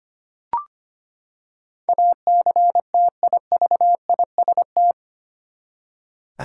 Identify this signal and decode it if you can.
Morse code, heard as a keyed tone, about 25 wpm, 700 Hz. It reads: ACTIVIST
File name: Eight Letter Words - 25wpm